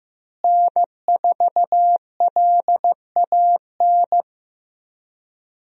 Morse code, heard as a keyed tone, about 15 wpm, 700 Hz. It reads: N4LAN
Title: USA Call Signs - 5 Characters - Repeated 2x - 15wpm